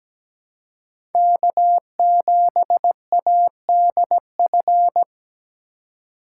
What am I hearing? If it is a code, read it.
Morse code, heard as a keyed tone, about 17 wpm, 700 Hz. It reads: K7ADF
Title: USA Call Signs - 5 Characters - 4x Character Spacing - 17wpm